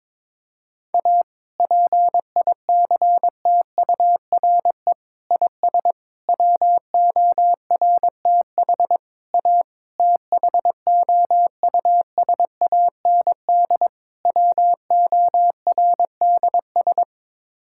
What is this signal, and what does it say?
Morse code, heard as a keyed tone, about 22 wpm, 700 Hz. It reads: A PICTURE IS WORTH A THOUSAND WORDS
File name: practice / 22wpm / Sets of 7 Words - 22wpm